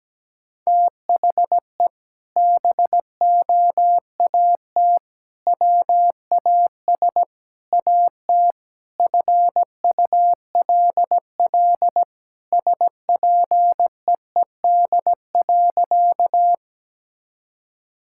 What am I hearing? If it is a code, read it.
Morse code, heard as a keyed tone, about 17 wpm, 700 Hz. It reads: THE BOAT WAS AT FULL SPEED.
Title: Sentences from Top 1000 Words - Review Part 5 - 17wpm